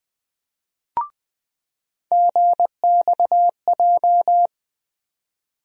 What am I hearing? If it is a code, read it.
Morse code, heard as a keyed tone, about 20 wpm, 700 Hz. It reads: GXJ
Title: USA Call Sign Postfixes - 3 Letters 20wpm